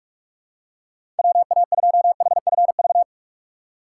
Morse code, heard as a keyed tone, about 43 wpm, 700 Hz. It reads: WA2HFV
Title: USA Call Signs - 6 Characters - 3x Character Spacing - 45wpm